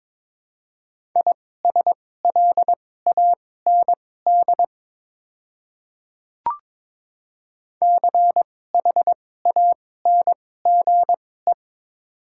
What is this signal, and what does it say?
Morse code, heard as a keyed tone, about 23 wpm, spaced as for 16 wpm, 700 Hz. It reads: ISLAND CHANGE
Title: 2x Character Spacing - Top 500 Words - 22wpm